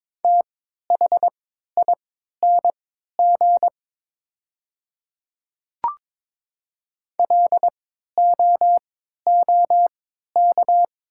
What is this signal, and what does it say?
Morse code, heard as a keyed tone, about 22 wpm, spaced as for 12 wpm, 690 Hz. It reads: THING LOOK
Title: 3x Character Spacing - Top 100 Words - 22wpm